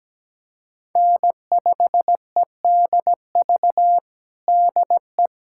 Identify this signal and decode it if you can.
Morse code, heard as a keyed tone, about 17 wpm, 700 Hz. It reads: N5EDV DE
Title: USA Callsign DE Callsign 17wpm